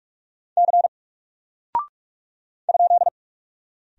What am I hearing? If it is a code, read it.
Morse code, heard as a keyed tone, about 44 wpm, 700 Hz. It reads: C ?
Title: Single Character (Letter:Number:Punctuation:Prosign) 45wpm